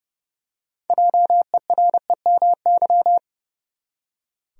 Morse code, heard as a keyed tone, about 30 wpm, 700 Hz. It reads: JEREMY
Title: USA Common Names - 30wpm